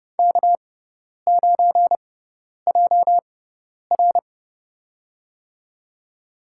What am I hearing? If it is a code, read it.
Morse code, heard as a keyed tone, about 30 wpm, spaced as for 10 wpm, 700 Hz. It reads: K9JR
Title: USA Call Signs - 4 Characters - 6x Character Spacing - 30wpm